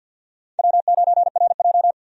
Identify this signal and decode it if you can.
Morse code, heard as a keyed tone, about 46 wpm, 700 Hz. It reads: W9RJ
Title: USA Call Signs - 4 Characters - 50wpm